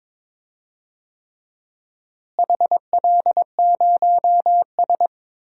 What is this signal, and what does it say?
Morse code, heard as a keyed tone, about 22 wpm, 700 Hz. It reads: HL0S
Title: DXCC-100 Call Signs - 4 Characters - Repeated 3x - 22wpm